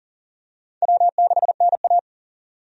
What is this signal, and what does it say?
Morse code, heard as a keyed tone, about 40 wpm, 700 Hz. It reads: W6NA
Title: USA Call Signs - 4 Characters - 40wpm